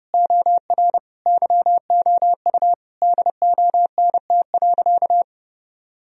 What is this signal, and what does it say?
Morse code, heard as a keyed tone, about 30 wpm, 700 Hz. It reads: OR YOU DONT.
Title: Sentences from Top 1000 Words - Review Part 4 - 30wpm